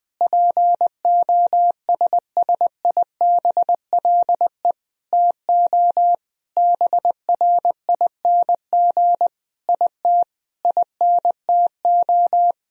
Morse code, heard as a keyed tone, about 20 wpm, 700 Hz. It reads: POSSIBLE TO BRING IT INTO
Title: Sentences from Top 1000 Words - Review Part 4 - 20wpm